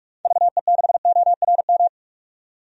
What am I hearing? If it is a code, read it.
Morse code, heard as a keyed tone, about 46 wpm, 700 Hz. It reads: VE6ORM
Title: DXCC-100 Call Signs - 6 Characters - 45wpm